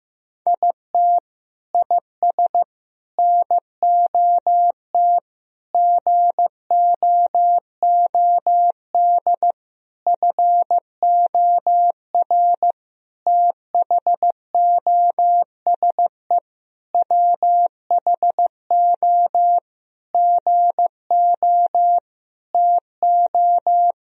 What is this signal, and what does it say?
Morse code, heard as a keyed tone, about 15 wpm, 700 Hz. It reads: IT IS NOT GOOD FOR THOSE WHO GO TO
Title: Sentences from Top 1000 Words - Review Part 2 - 15wpm